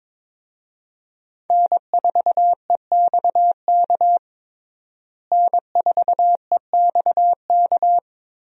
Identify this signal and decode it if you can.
Morse code, heard as a keyed tone, about 21 wpm, 700 Hz. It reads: N4EXK N4EXK
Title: USA Call Signs - 5 Characters - Repeated 2x - 22wpm